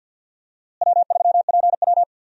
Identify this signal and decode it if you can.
Morse code, heard as a keyed tone, about 47 wpm, 700 Hz. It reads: W3PW
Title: USA Call Signs - 4 Characters - Repeated 3x - 50wpm